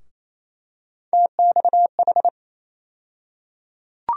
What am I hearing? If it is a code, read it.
Morse code, heard as a keyed tone, about 28 wpm, 700 Hz. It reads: TXH
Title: USA Call Sign Postfixes - 3 Letters 28wpm